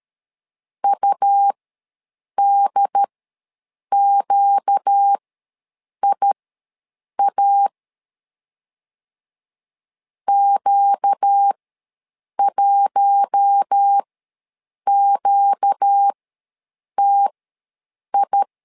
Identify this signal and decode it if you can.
Morse code, heard as a keyed tone, about 13 wpm, spaced as for 7 wpm, 800 Hz. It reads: UDQIA Q1QTI
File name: LEZIONI / 4 Q1 ws 12wpm 2129